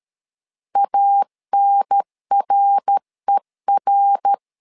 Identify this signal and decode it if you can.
Morse code, heard as a keyed tone, about 13 wpm, 800 Hz. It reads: ANRER
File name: LEZIONI / 3 MWRO / 3 MWR 12wpm 1711